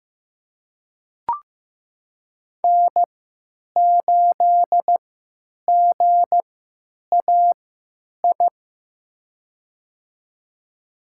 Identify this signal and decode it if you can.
Morse code, heard as a keyed tone, about 15 wpm, spaced as for 9 wpm, 700 Hz. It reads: N8GAI
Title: USA Call Signs - 5 Characters - 3x Character Spacing - 15wpm